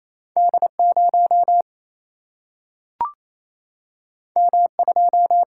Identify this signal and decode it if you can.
Morse code, heard as a keyed tone, about 28 wpm, 700 Hz. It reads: D 0 M 2
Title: Single-Letter Single-Number 28wpm